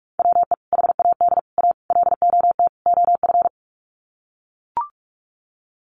Morse code, heard as a keyed tone, about 44 wpm, 700 Hz. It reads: WE HAD A LOT OF
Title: Sets of 5 Words Encore 45wpm